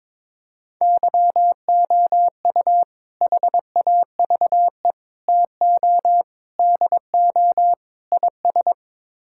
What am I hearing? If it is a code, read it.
Morse code, heard as a keyed tone, about 22 wpm, 700 Hz. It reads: YOU HAVE TO DO IS
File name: practice / 22wpm / Build A Phrase - 5-Word Phrases - 22wpm